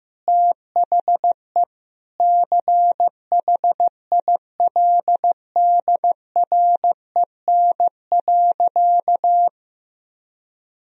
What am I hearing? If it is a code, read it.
Morse code, heard as a keyed tone, about 15 wpm, 700 Hz. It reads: THE CHILDREN.
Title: Sentences from Top 300 Words and Abbreviations 15wpm